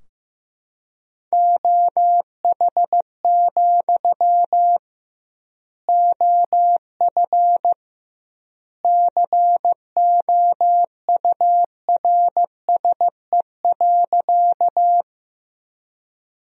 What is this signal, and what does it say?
Morse code, heard as a keyed tone, about 15 wpm, 700 Hz. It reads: OH, OF COURSE.
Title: 2x Word-Spacing - Sentences from Top 500 Words - 15wpm